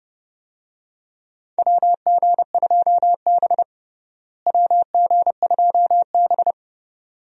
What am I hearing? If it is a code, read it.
Morse code, heard as a keyed tone, about 31 wpm, 700 Hz. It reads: WG2B WG2B
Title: USA Call Signs - 4 Characters - Repeated 2x - 30wpm